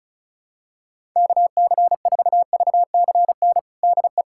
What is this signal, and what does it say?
Morse code, heard as a keyed tone, about 34 wpm, 700 Hz. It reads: KC4VCN DE
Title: USA Callsign DE Callsign - 6 Characters - 35wpm